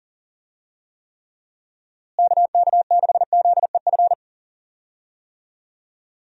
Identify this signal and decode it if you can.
Morse code, heard as a keyed tone, about 40 wpm, 700 Hz. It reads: KK6ZEF
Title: USA Call Signs - 6 Characters - Repeated 3x - 40wpm